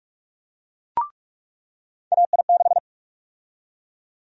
Morse code, heard as a keyed tone, about 43 wpm, 700 Hz. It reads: AI6
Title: USA Call Sign Prefixes - 45wpm